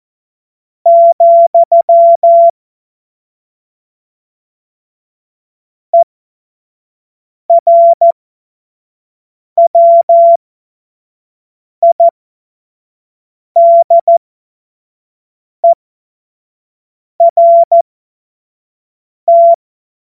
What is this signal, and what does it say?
Morse code, heard as a keyed tone, about 14 wpm, spaced as for 5 wpm, 680 Hz. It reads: , ERWIDERT
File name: Morsekurs / Witze / Witz 04. Januar 2022, 25 F70